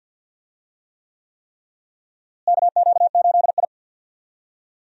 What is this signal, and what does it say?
Morse code, heard as a keyed tone, about 51 wpm, 700 Hz. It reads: KQ8I
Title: USA Call Signs - 4 Characters - Repeated 3x - 50wpm